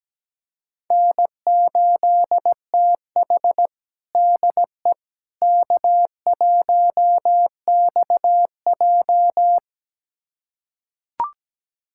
Morse code, heard as a keyed tone, about 17 wpm, 700 Hz. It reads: N8TH DE K1XJ
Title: USA Callsign DE Callsign - 4 Characters - 17wpm